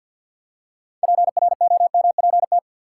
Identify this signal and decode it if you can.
Morse code, heard as a keyed tone, about 53 wpm, 700 Hz. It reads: PROMPT
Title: Six Letter Words - 50wpm